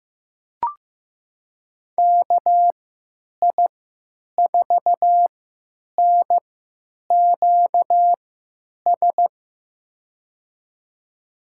Morse code, heard as a keyed tone, about 15 wpm, spaced as for 9 wpm, 700 Hz. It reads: KI4NQS
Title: USA Call Signs - 6 Characters - 3x Character Spacing - 15wpm